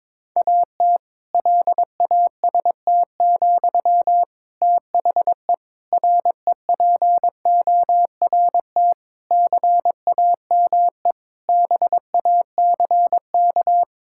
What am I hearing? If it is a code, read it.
Morse code, heard as a keyed tone, about 21 wpm, 700 Hz. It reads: AT LAST, THE REPORT CAME BACK
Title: Sentences from Top 1000 Words - Review Part 6 - 22wpm